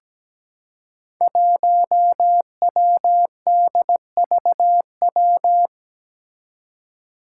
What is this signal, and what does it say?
Morse code, heard as a keyed tone, about 17 wpm, 700 Hz. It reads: 1WDVW
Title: Five Challenging Letter-Number Combinations 17wpm